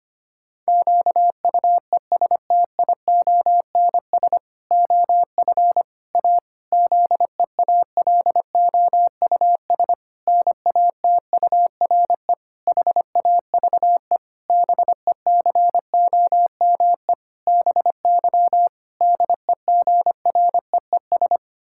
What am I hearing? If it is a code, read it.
Morse code, heard as a keyed tone, about 25 wpm, 700 Hz. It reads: QUESTIONS OF A ZEALOUS NATURE HAVE BECOME BY DEGREES
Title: Pangram Challenge v2 - 25wpm